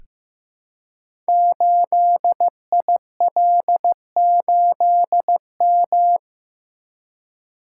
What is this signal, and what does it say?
Morse code, heard as a keyed tone, about 15 wpm, 700 Hz. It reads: 8IL8M
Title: Code Groups - 5 Letter:Numbers - 15wpm